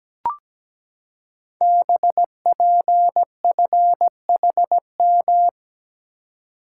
Morse code, heard as a keyed tone, about 17 wpm, 700 Hz. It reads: BPFHM
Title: Code Groups - 5 Letters - 17wpm